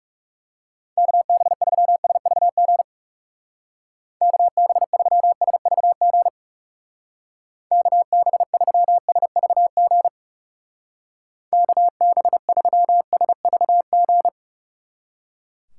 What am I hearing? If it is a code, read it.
Morse code, heard as a keyed tone, about 38 wpm, 700 Hz. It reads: KB3SVG KB3SVG KB3SVG KB3SVG